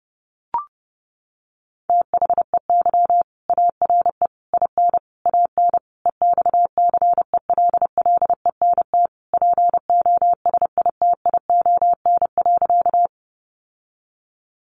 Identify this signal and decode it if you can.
Morse code, heard as a keyed tone, about 30 wpm, 700 Hz. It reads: THEY ARE IN AN EXCELLENT POSITION.
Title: Sentences from Top 1000 Words 30wpm